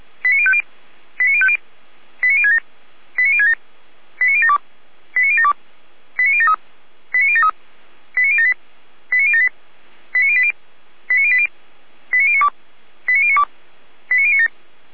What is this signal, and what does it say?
Five-tone selective call (ZVEI), heard as quick runs of five tones, tones of about 70 ms. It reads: 89050 89050 89067 89067 89072 89072 89073 89073 89078 89078 89080 89080 89092 89092 89097